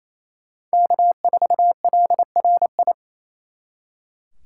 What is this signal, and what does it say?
Morse code, heard as a keyed tone, about 34 wpm, 700 Hz. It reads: K4LRI
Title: Speed Racing - USA Call Signs 28wpm